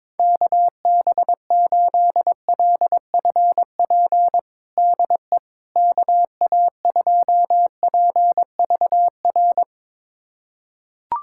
Morse code, heard as a keyed tone, about 21 wpm, 700 Hz. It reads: KB8LFP DE KA2PVR
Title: USA Callsign DE Callsign - 6 Characters - 22wpm